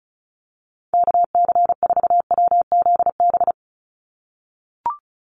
Morse code, heard as a keyed tone, about 34 wpm, 700 Hz. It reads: KC4WZB
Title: NA SOTA Activator Callsigns 35wpm